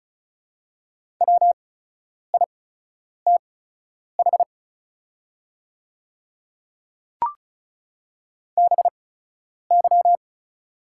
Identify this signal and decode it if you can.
Morse code, heard as a keyed tone, about 36 wpm, spaced as for 10 wpm, 700 Hz. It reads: WITH BY
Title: 8x Character Spacing - Top 100 Words - 35wpm